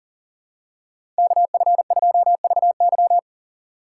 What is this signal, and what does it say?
Morse code, heard as a keyed tone, about 40 wpm, 700 Hz. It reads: KF2VY
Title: USA Call Signs - 5 Characters - 2x Character Spacing - 40wpm